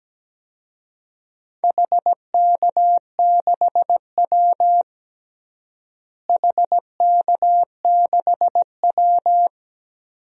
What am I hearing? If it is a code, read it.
Morse code, heard as a keyed tone, about 17 wpm, 700 Hz. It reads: HK6W HK6W
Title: DXCC-100 Call Signs - 4 Characters - Repeated 2x - 17wpm